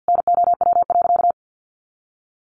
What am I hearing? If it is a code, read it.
Morse code, heard as a keyed tone, about 50 wpm, 700 Hz. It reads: NOW.
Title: A Sound Of Thunder 50wpm